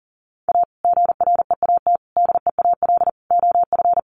Morse code, heard as a keyed tone, about 40 wpm, 700 Hz. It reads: A GREAT DEAL OF
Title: Sets of 4 Words 40wpm